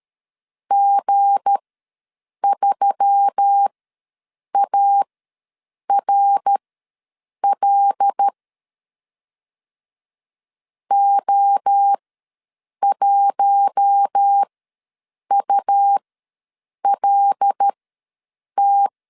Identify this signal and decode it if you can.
Morse code, heard as a keyed tone, about 13 wpm, spaced as for 7 wpm, 800 Hz. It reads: G3ARL O1ULT
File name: LEZIONI / 7 GF9Z / 7 G ws 12wpm 3140